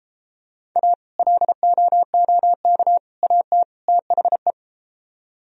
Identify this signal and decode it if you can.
Morse code, heard as a keyed tone, about 34 wpm, 700 Hz. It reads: A LOOK AT THE